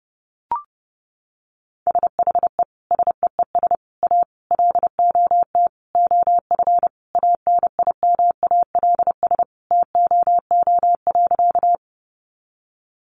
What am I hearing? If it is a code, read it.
Morse code, heard as a keyed tone, about 30 wpm, 700 Hz. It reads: SHE SEES A LOT OF ANIMALS TOO.
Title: Easy Phrases 30wpm